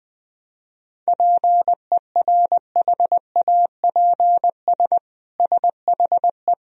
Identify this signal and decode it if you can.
Morse code, heard as a keyed tone, about 20 wpm, 700 Hz. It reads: PERHAPS SHE
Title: Sentences from Top 1000 Words - Review Part 1 - 20wpm